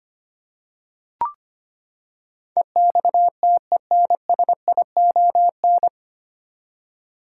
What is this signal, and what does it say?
Morse code, heard as a keyed tone, about 25 wpm, 700 Hz. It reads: EXTENSION
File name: Nine Letter Words - 25wpm